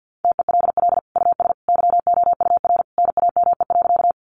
Morse code, heard as a keyed tone, about 53 wpm, 700 Hz. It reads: TELL US YOUR NAME.
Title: Sentences from Top 200 Words 50wpm